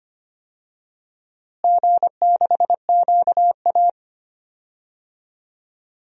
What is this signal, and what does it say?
Morse code, heard as a keyed tone, about 25 wpm, 700 Hz. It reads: G6QA
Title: DXCC-100 Call Signs - 4 Characters - Repeated 3x - 25wpm